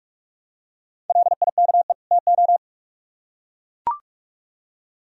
Morse code, heard as a keyed tone, about 44 wpm, 700 Hz. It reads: LIKE TO